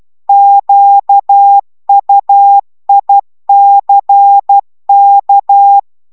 Morse code, heard as a keyed tone, about 12 wpm, 800 Hz. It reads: QUICK
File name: CW 10 baud